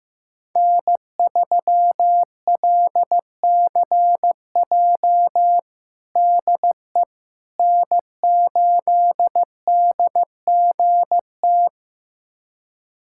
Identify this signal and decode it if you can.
Morse code, heard as a keyed tone, about 15 wpm, 700 Hz. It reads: N3LCJ DE N8DGT
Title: USA Callsign DE Callsign 15wpm